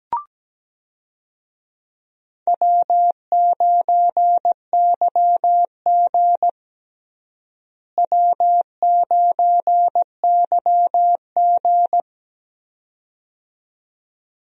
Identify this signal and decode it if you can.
Morse code, heard as a keyed tone, about 17 wpm, 700 Hz. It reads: W9YG W9YG
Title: USA Call Signs - 4 Characters - Repeated 2x - 17wpm